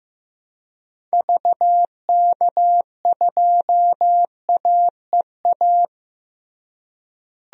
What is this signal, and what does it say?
Morse code, heard as a keyed tone, about 15 wpm, 700 Hz. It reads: VK2AEA
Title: DXCC-100 Call Signs - 6 Characters - Repeated 2x - 15wpm